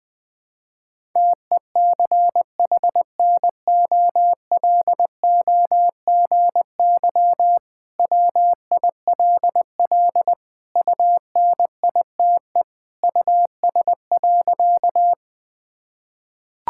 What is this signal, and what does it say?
Morse code, heard as a keyed tone, about 21 wpm, 700 Hz. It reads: TECHNOLOGY WILL UNITE US.
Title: Sentences from Top 1000 Words - Review Part 5 - 20wpm